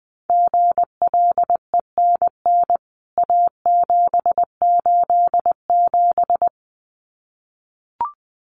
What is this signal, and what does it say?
Morse code, heard as a keyed tone, about 19 wpm, 700 Hz. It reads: GLENN A787
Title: CWT Exchange 20wpm